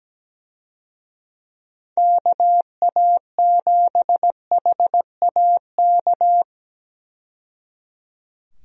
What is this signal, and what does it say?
Morse code, heard as a keyed tone, about 17 wpm, 700 Hz. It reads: KA7HAK
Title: USA Call Signs - 6 Characters - 17wpm